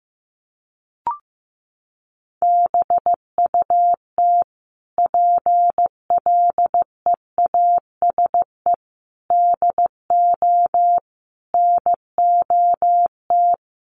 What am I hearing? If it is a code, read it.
Morse code, heard as a keyed tone, about 14 wpm, 700 Hz. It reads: BUT PLEASE DO NOT
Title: Sentences from Top 1500 Words 15wpm